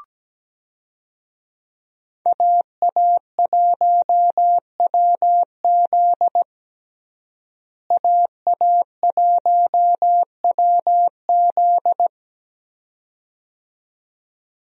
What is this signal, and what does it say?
Morse code, heard as a keyed tone, about 17 wpm, 700 Hz. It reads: AA1WZ AA1WZ
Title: USA Call Signs - 5 Characters - Repeated 2x - 17wpm